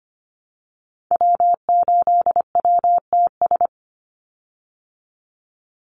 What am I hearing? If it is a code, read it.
Morse code, heard as a keyed tone, about 25 wpm, 700 Hz. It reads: W8WTS
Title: Call Signs Repeated 2x - 25wpm